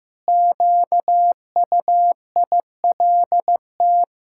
Morse code, heard as a keyed tone, about 15 wpm, 700 Hz. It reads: QUILT
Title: Pangram Challenge v2 - 15wpm